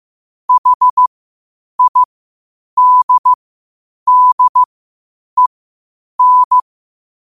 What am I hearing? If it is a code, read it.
Morse code, heard as a keyed tone, about 15 wpm, spaced as for 8 wpm, 1000 Hz. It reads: HIDDEN
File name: qotdfiles / 8wpm